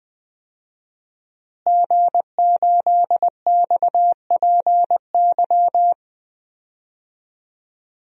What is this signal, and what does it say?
Morse code, heard as a keyed tone, about 20 wpm, 700 Hz. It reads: G8XPY
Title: DXCC-100 Call Signs - 5 Characters - Repeated 3x - 20wpm